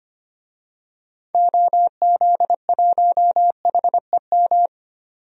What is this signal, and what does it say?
Morse code, heard as a keyed tone, about 25 wpm, 700 Hz. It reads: OZ1HEM
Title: DXCC-100 Call Signs - 6 Characters - 25wpm